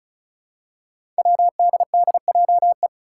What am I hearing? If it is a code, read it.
Morse code, heard as a keyed tone, about 34 wpm, 700 Hz. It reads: WDDJE
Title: Five Challenging Letter-Number Combinations 35wpm